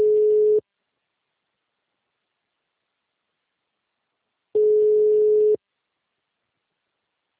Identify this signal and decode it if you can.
Ringback tone (Europe, 425 Hz).
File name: out- -2001- - - .150